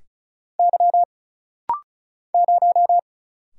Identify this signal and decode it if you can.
Morse code, heard as a keyed tone, about 34 wpm, 700 Hz. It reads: Y 0
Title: Single Letter-Number - Rapid-Fire 35wpm